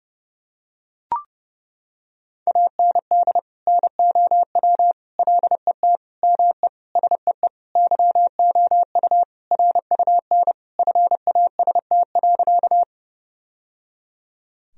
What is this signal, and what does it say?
Morse code, heard as a keyed tone, about 30 wpm, 700 Hz. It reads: AND NOW LET ME SEE YOU RUN FAST.
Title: Sentences from Top 1000 Words - Review Part 2 - 30wpm